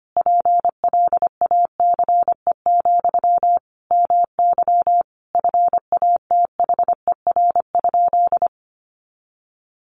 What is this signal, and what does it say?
Morse code, heard as a keyed tone, about 25 wpm, 700 Hz. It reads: PLACE, MY FATHER?
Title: Sentences from Top 400 Words 25wpm